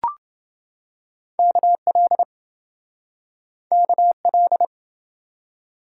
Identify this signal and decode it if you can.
Morse code, heard as a keyed tone, about 27 wpm, 700 Hz. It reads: KL KL